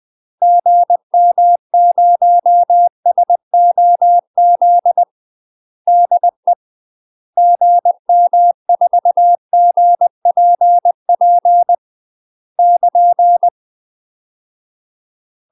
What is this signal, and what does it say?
Morse code, heard as a keyed tone, about 20 wpm, 700 Hz. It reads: GM0SOZ DE GM4GPP <KN>
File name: UK MASTER QSOs TRAINING / 20WPM / UK MASTER QSOs 0048